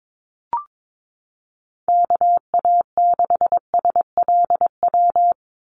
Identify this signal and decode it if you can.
Morse code, heard as a keyed tone, about 22 wpm, 700 Hz. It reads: KA6SLW
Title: USA Call Signs 22wpm